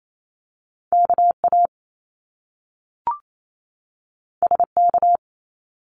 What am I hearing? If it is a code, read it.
Morse code, heard as a keyed tone, about 28 wpm, 700 Hz. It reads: KA SK
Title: Top Two Letter Combinations in Words 28wpm